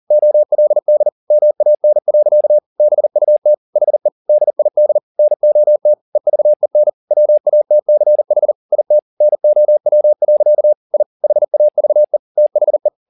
Morse code, heard as a keyed tone, about 40 wpm, 600 Hz. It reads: OLD MAN. BUT HE DID NOT EVEN WATCH IT NOW. I HAVE THE
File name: the old man and the sea 2 40wpm